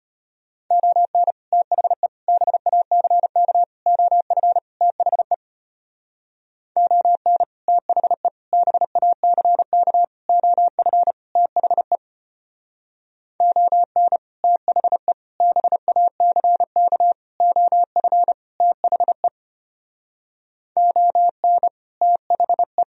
Morse code, heard as a keyed tone, about 31 wpm, 700 Hz. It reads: ON THE BACK OF THE ON THE BACK OF THE ON THE BACK OF THE ON THE